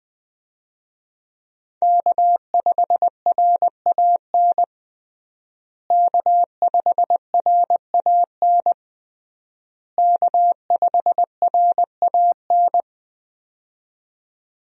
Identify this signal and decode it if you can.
Morse code, heard as a keyed tone, about 21 wpm, 700 Hz. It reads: K5RAN K5RAN K5RAN
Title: USA Call Signs - 5 Characters - Repeated 3x - 20wpm